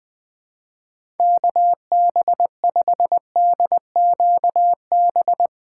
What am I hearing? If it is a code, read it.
Morse code, heard as a keyed tone, about 20 wpm, 700 Hz. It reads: KB5DQB
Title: Speed Racing - USA Call Signs 20wpm